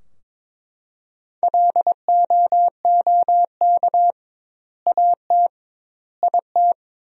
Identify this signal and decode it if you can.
Morse code, heard as a keyed tone, about 22 wpm, 700 Hz. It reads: LOOK AT IT
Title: 2x Word-Spacing - Sentences from Top 200 Words - 22wpm